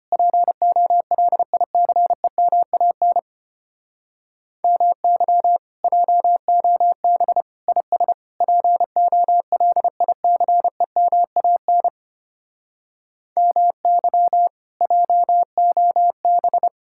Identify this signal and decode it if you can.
Morse code, heard as a keyed tone, about 33 wpm, 700 Hz. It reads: POLICEMAN MY JOB IS POLICEMAN MY JOB